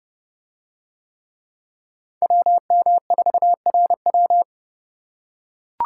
Morse code, heard as a keyed tone, about 30 wpm, 700 Hz. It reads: WM4RW